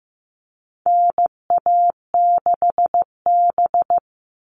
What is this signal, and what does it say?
Morse code, heard as a keyed tone, about 15 wpm, 700 Hz. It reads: NA6B
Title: All Cut Numbers - 4-Digit 15wpm